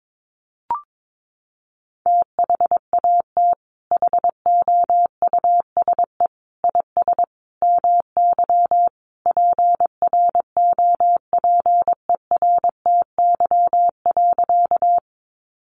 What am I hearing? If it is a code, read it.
Morse code, heard as a keyed tone, about 22 wpm, 700 Hz. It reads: THAT HOUSE IS MY PROPERTY.
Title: Sentences from Top 1000 Words 22wpm